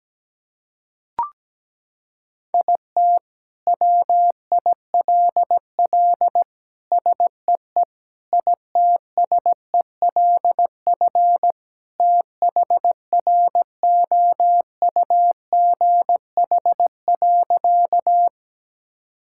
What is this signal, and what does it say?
Morse code, heard as a keyed tone, about 17 wpm, 700 Hz. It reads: IT WILL SEE ITSELF THROUGH.
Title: Sentences from Top 1000 Words - Review Part 5 - 17wpm